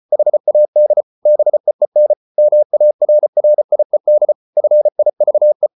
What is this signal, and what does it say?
Morse code, heard as a keyed tone, about 34 wpm, 600 Hz. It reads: HAD BEEN MARRIED FIVE